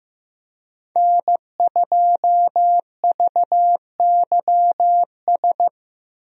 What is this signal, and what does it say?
Morse code, heard as a keyed tone, about 15 wpm, 700 Hz. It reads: N2VYS
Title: USA Call Signs - 5 Characters - Repeated 2x - 15wpm